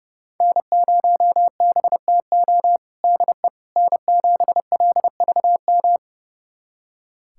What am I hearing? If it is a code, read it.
Morse code, heard as a keyed tone, about 29 wpm, 700 Hz. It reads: N0BTO DE N7LVM
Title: USA Callsign DE Callsign - 5 Characters - 30wpm